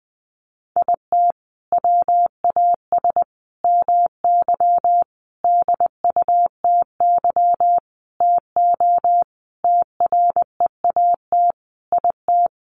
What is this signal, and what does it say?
Morse code, heard as a keyed tone, about 20 wpm, 700 Hz. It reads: IT WAS MY DUTY TO TREAT IT
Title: Sentences from Top 1500 Words 20wpm